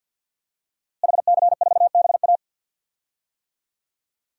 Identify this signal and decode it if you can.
Morse code, heard as a keyed tone, about 49 wpm, 700 Hz. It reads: HC4BA
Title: DXCC-100 Call Signs - 5 Characters - Repeated 3x - 50wpm